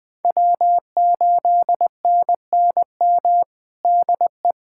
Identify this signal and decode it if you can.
Morse code, heard as a keyed tone, about 21 wpm, 700 Hz. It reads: W8NNM DE
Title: USA Callsign DE Callsign - 5 Characters - 20wpm